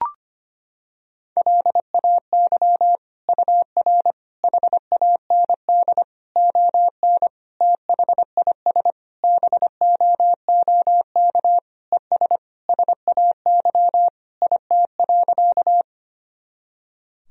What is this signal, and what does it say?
Morse code, heard as a keyed tone, about 25 wpm, 700 Hz. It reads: LAY UR HAND ON THIS BOOK ES SAY IT.
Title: Sentences from Top 400 Words and Abbreviations 25wpm